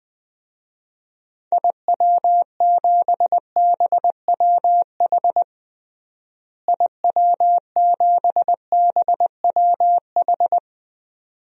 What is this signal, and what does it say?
Morse code, heard as a keyed tone, about 20 wpm, 700 Hz. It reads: IW7BWH IW7BWH
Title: DXCC-100 Call Signs - 6 Characters - Repeated 2x - 20wpm